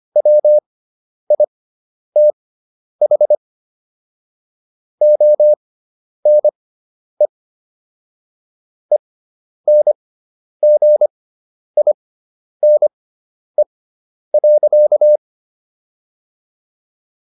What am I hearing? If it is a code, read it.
Morse code, heard as a keyed tone, about 25 wpm, spaced as for 10 wpm, 600 Hz. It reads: WITH ONE ENGINE.